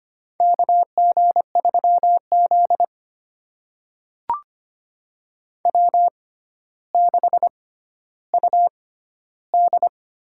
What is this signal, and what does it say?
Morse code, heard as a keyed tone, about 25 wpm, spaced as for 13 wpm, 700 Hz. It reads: KG3Z W6UD
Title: USA Call Signs - 6x Character Spacing - 25wpm